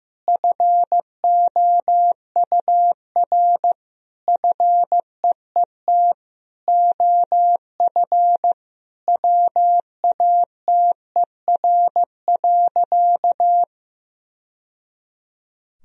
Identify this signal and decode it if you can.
Morse code, heard as a keyed tone, about 16 wpm, 700 Hz. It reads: FOUR FEET OF WATER.
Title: Sentences from Top 1000 Words - Review Part 1 - 15wpm